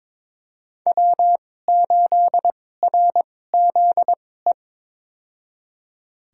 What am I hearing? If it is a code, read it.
Morse code, heard as a keyed tone, about 22 wpm, spaced as for 16 wpm, 700 Hz. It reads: W8RZE
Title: USA Call Signs - 5 Characters - 2x Character Spacing - 22wpm